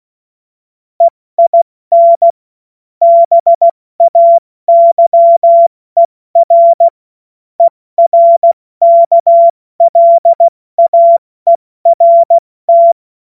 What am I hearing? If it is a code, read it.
Morse code, heard as a keyed tone, about 16 wpm, 680 Hz. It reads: EIN BAYER ERKLAERT
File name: Witz-05-August-2025-70-F80